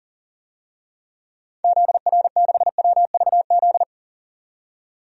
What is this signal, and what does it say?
Morse code, heard as a keyed tone, about 40 wpm, 700 Hz. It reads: ZR6WVZ
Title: DXCC-100 Call Signs - 6 Characters - 40wpm